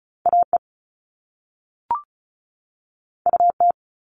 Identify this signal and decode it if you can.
Morse code, heard as a keyed tone, about 36 wpm, 690 Hz. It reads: AE UT
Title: All Cut Numbers - 2-Digit 35wpm